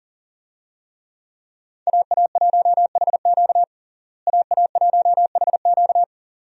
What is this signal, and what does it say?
Morse code, heard as a keyed tone, about 40 wpm, 700 Hz. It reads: AA1HQ AA1HQ
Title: USA Call Signs - 5 Characters - Repeated 2x - 40wpm